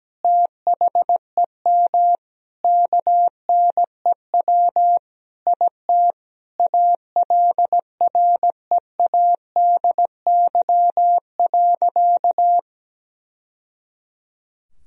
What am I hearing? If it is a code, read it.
Morse code, heard as a keyed tone, about 17 wpm, 700 Hz. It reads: THEM KNEW IT ALREADY.
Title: Sentences from Top 1000 Words - Review Part 3 - 17wpm